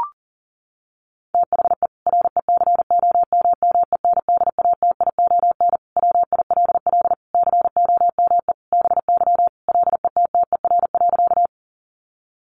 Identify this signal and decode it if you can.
Morse code, heard as a keyed tone, about 37 wpm, 700 Hz. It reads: THE RECOMMENDATION WILL COME BY LETTER.
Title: Sentences from Top 1400 Words 40wpm